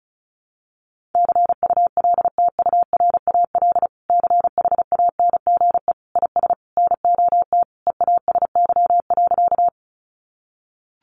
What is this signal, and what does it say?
Morse code, heard as a keyed tone, about 34 wpm, 700 Hz. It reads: CULTURAL CHANGE IS NOT EASY.
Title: Sentences from Top 1500 Words 35wpm